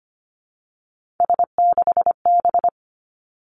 Morse code, heard as a keyed tone, about 25 wpm, 700 Hz. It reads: S6B
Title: Three Challenging Letter-Number Combinations 25wpm